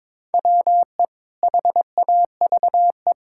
Morse code, heard as a keyed tone, about 21 wpm, 700 Hz. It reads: WE HAVE
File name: practice / 22wpm / Sets of 7 Words - 22wpm